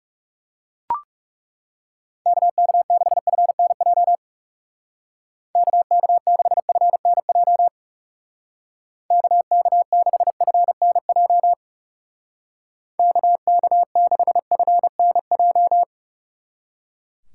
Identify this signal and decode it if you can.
Morse code, heard as a keyed tone, about 38 wpm, 700 Hz. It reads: KK6FNJ KK6FNJ KK6FNJ KK6FNJ
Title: Speed Racing - USA 2x3 Call Signs 30wpm